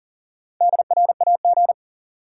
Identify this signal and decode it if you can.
Morse code, heard as a keyed tone, about 40 wpm, 700 Hz. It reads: DRAG
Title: Sentences from Top 1500 Words - New Word Drill - 40wpm